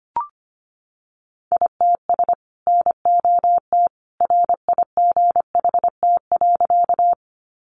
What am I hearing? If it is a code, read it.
Morse code, heard as a keyed tone, about 25 wpm, 700 Hz. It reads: ITS NOT RIGHT.
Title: Sentences from Top 600 Words 25wpm